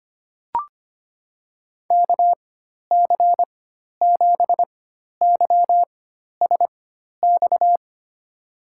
Morse code, heard as a keyed tone, about 25 wpm, spaced as for 12 wpm, 700 Hz. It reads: KC7YSX
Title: USA Call Signs - 6 Characters - 4x Character Spacing - 25wpm